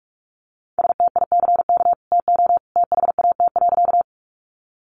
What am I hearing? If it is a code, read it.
Morse code, heard as a keyed tone, about 44 wpm, 700 Hz. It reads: STICK TO THAT.
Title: Sentences from Top 1200 Words 45wpm